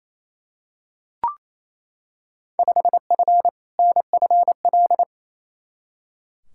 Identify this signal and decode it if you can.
Morse code, heard as a keyed tone, about 28 wpm, 700 Hz. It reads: 5F NFL
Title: Field Day Exchanges Encore - 28wpm